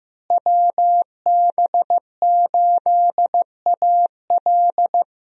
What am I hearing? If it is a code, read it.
Morse code, heard as a keyed tone, about 15 wpm, 700 Hz. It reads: WB8AL
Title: USA Call Signs - 5 Characters - Repeated 2x - 15wpm